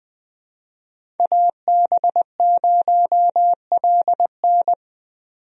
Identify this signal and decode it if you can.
Morse code, heard as a keyed tone, about 20 wpm, 700 Hz. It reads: AB0LN
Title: USA Call Signs - 5 Characters - 20wpm